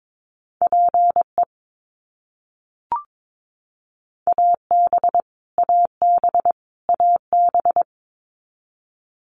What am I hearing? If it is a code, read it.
Morse code, heard as a keyed tone, about 22 wpm, 700 Hz. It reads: PE AB AB AB
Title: Canadian Provinces and Territories - Repeated 3x - 22wpm